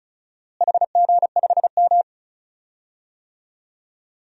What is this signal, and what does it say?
Morse code, heard as a keyed tone, about 36 wpm, 700 Hz. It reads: HG5M
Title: DXCC-100 Call Signs - 4 Characters - Repeated 2x - 35wpm